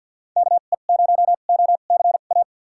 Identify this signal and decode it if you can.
Morse code, heard as a keyed tone, about 52 wpm, spaced as for 37 wpm, 700 Hz. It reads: KE0OCA
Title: USA Call Signs - 2x Character Spacing - 50wpm